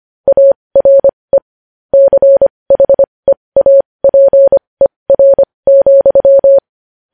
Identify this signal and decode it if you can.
Morse code, heard as a keyed tone, about 25 wpm, spaced as for 20 wpm, 550 Hz. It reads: ARE CHEAPER,
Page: News Headlines in Morse Code at 20 WPM